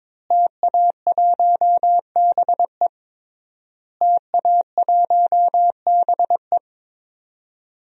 Morse code, heard as a keyed tone, about 21 wpm, 700 Hz. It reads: TA1BE TA1BE
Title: DXCC-100 Call Signs - 5 Characters - Repeated 3x - 22wpm